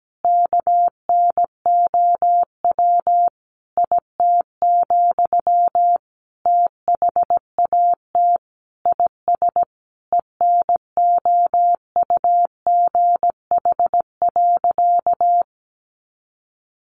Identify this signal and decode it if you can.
Morse code, heard as a keyed tone, about 17 wpm, 700 Hz. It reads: KNOW IT, THAT IS ENOUGH.
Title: Sentences from Top 400 Words 17wpm